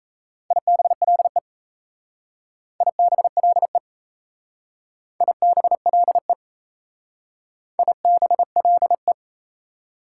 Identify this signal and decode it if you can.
Morse code, heard as a keyed tone, about 34 wpm, 700 Hz. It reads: IBLE IBLE IBLE IBLE
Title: Speed-Racing Prefixes and Suffixes 28wpm